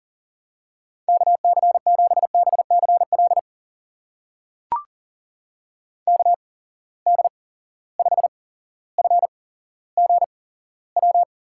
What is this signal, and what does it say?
Morse code, heard as a keyed tone, about 40 wpm, spaced as for 17 wpm, 700 Hz. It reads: KC7BCL KD5FGW
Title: USA Call Signs - 6 Characters - 8x Character Spacing - 40wpm